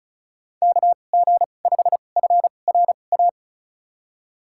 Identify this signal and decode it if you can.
Morse code, heard as a keyed tone, about 36 wpm, spaced as for 25 wpm, 700 Hz. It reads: KG5FRA
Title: USA Call Signs - 6 Characters - 2x Character Spacing - 35wpm